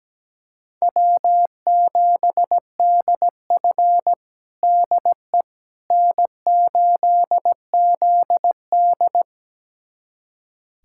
Morse code, heard as a keyed tone, about 17 wpm, 700 Hz. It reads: W7DF DE N8ZD
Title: USA Callsign DE Callsign - 4 Characters - 17wpm